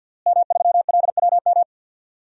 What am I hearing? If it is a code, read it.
Morse code, heard as a keyed tone, about 46 wpm, 700 Hz. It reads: M3LWM
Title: DXCC-100 Call Signs - 5 Characters - Repeated 2x - 50wpm